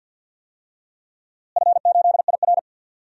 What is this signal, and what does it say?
Morse code, heard as a keyed tone, about 52 wpm, 700 Hz. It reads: F8IR
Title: DXCC-100 Call Signs - 4 Characters - Repeated 2x - 50wpm